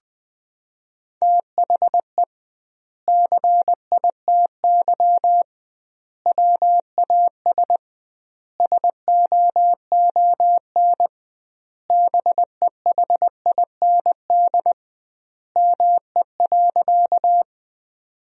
Morse code, heard as a keyed tone, about 20 wpm, 700 Hz. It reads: THE CITY WAS SOON BEHIND ME.
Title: 2x Word-Spacing - Sentences from Top 500 Words - 20wpm